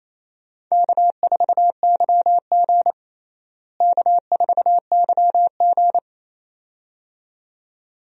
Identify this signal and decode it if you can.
Morse code, heard as a keyed tone, about 28 wpm, 700 Hz. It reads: K4YG K4YG
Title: USA Call Signs - 4 Characters - Repeated 2x - 28wpm